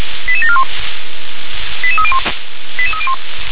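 Five-tone selective call (ZVEI), heard as quick runs of five tones, tones of about 70 ms. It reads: 88741 88301 88301
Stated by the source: ZVEI five-tone selective call received by SDR.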